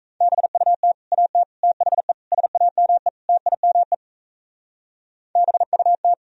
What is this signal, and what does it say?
Morse code, heard as a keyed tone, about 45 wpm, 700 Hz. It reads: BUT AT THE SAME TIME BUT